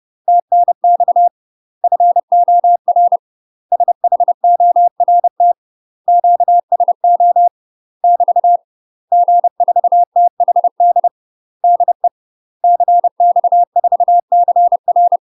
Morse code, heard as a keyed tone, about 30 wpm, 700 Hz. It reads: TNX FOR SHORT QSO <BT> G4THD DE CX4CR
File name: UK MASTER QSOs TRAINING / 30WPM / UK MASTER QSOs 0022